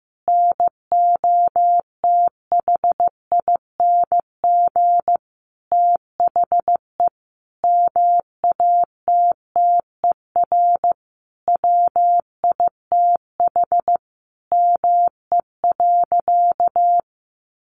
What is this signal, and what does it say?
Morse code, heard as a keyed tone, about 15 wpm, 700 Hz. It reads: NOTHING THE MATTER WITH ME.
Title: Sentences from Top 700 Words 15wpm